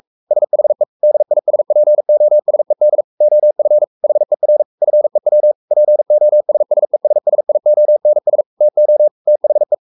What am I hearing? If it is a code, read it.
Morse code, heard as a keyed tone, about 44 wpm, 600 Hz. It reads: SHE DISPOSED OF HER FEW POSSESSIONS TO THE
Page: Romanen Never af Ken Follet på engelsk.